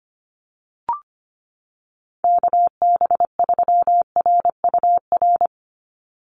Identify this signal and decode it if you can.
Morse code, heard as a keyed tone, about 25 wpm, 700 Hz. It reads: KB3RUR
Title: USA Call Signs 25wpm